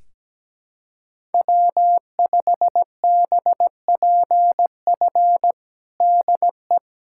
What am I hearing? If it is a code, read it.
Morse code, heard as a keyed tone, about 17 wpm, 700 Hz. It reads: W5BPF DE
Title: USA Callsign DE Callsign 17wpm